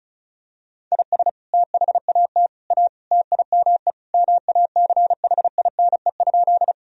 Morse code, heard as a keyed tone, about 36 wpm, 700 Hz. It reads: IS THAT A TIME MACHINE?
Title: Sentences from Top 1000 Words - Review Part 2 - 35wpm